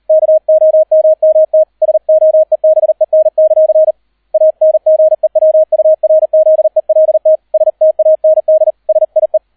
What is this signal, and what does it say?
Morse code, heard as a keyed tone, about 38 wpm, 630 Hz, with radio noise. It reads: KOMMT SOEBEN; ANGEWURZELT STAND SIE